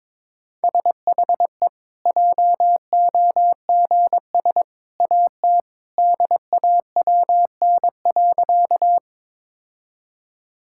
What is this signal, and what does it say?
Morse code, heard as a keyed tone, about 22 wpm, 700 Hz. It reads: SHE JOGS AT DAWN.
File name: practice / 22wpm / Easy Sentences with Hints - 22wpm